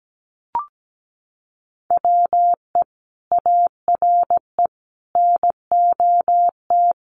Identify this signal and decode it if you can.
Morse code, heard as a keyed tone, about 17 wpm, 700 Hz. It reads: WE ARE NOT
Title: Sets of 5 Words 17wpm